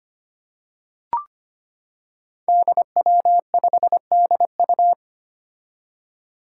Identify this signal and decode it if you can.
Morse code, heard as a keyed tone, about 25 wpm, 700 Hz. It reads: DW5DU
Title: Five Challenging Letter-Number Combinations 25wpm